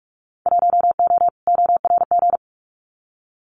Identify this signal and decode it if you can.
Morse code, heard as a keyed tone, about 44 wpm, 700 Hz. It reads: 1O ORG
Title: Winter Field Day Exchanges 45wpm